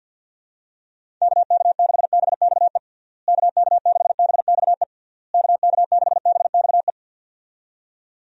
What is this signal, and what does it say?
Morse code, heard as a keyed tone, about 53 wpm, 700 Hz. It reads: KK6BXE KK6BXE KK6BXE
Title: USA Call Signs - 6 Characters - Repeated 3x - 50wpm